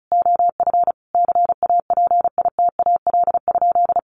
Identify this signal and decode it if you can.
Morse code, heard as a keyed tone, about 34 wpm, 700 Hz. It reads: OF CAPITAL?
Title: Sentences from Top 1100 Words 35wpm